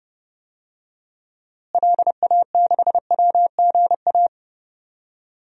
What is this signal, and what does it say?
Morse code, heard as a keyed tone, about 31 wpm, 700 Hz. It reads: LA6WGA
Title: DXCC-100 Call Signs - 6 Characters - Repeated 2x - 30wpm